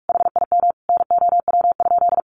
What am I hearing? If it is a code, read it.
Morse code, heard as a keyed tone, about 44 wpm, 700 Hz. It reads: HIM NOW?
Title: Sentences from Top 1400 Words 45wpm